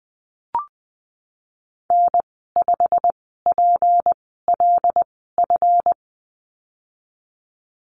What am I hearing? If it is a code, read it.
Morse code, heard as a keyed tone, about 20 wpm, spaced as for 14 wpm, 700 Hz. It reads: N5PLF
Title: 161 Lesson 26g - Letter g - Call Signs 20wpm